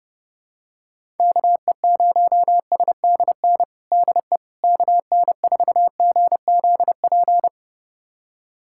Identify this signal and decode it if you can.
Morse code, heard as a keyed tone, about 32 wpm, 700 Hz. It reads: KE0SDN DE KN4GZP
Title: USA Callsign DE Callsign - 6 Characters - 30wpm